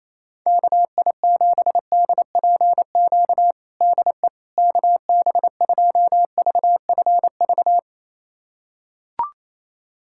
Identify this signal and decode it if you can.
Morse code, heard as a keyed tone, about 28 wpm, 700 Hz. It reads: KI7DPQ DE KB2VFV
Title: USA Callsign DE Callsign - 6 Characters - 28wpm